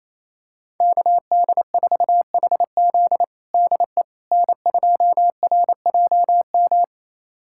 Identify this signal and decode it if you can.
Morse code, heard as a keyed tone, about 28 wpm, 700 Hz. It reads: KD4HZ DE N2RJM
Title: USA Callsign DE Callsign - 5 Characters - 28wpm